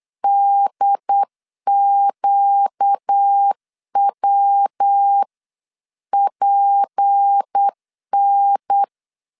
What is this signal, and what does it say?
Morse code, heard as a keyed tone, about 8 wpm, 800 Hz. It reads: DQWPN
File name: LEZIONI / 6 PKX0 / 6 PKX0 8wpm 3341